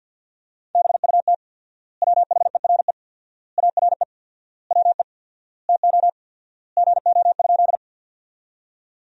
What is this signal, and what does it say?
Morse code, heard as a keyed tone, about 50 wpm, 700 Hz. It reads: BUT WHERE ARE WE TO GO?
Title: 4x Word-Spacing - Sentences from Top 200 Words - 50wpm